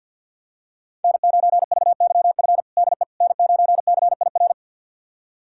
Morse code, heard as a keyed tone, about 52 wpm, 700 Hz. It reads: N9VYF DE N9CIR
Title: USA Callsign DE Callsign - 5 Characters - 50wpm